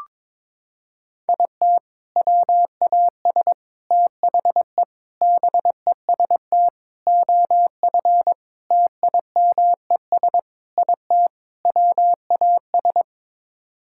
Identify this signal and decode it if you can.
Morse code, heard as a keyed tone, about 22 wpm, 700 Hz. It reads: IT WAS THE BEST OF TIMES IT WAS
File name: practice / 22wpm / Sets of 8 Words - 22wpm